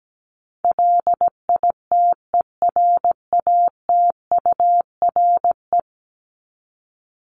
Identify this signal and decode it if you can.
Morse code, heard as a keyed tone, about 17 wpm, 700 Hz. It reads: LITERATURE
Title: Top 501-750 Words 17wpm